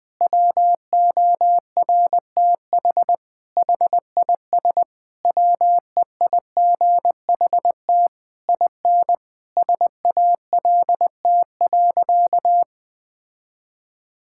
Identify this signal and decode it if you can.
Morse code, heard as a keyed tone, about 19 wpm, 700 Hz. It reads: WORTH HIS WEIGHT IN SALT.
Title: Sentences from Top 1000 Words - Review Part 5 - 20wpm